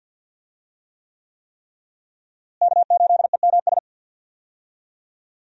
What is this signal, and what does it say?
Morse code, heard as a keyed tone, about 52 wpm, 700 Hz. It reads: K8EMS
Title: USA Call Signs - 5 Characters - Repeated 3x - 50wpm